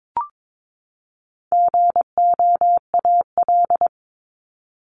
Morse code, heard as a keyed tone, about 22 wpm, 700 Hz. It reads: GOAL
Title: Top 751-1000 Words 22wpm